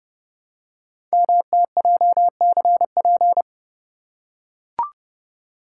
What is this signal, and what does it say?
Morse code, heard as a keyed tone, about 30 wpm, 700 Hz. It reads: MTJCP
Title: Code Groups - 5 Letters - 30wpm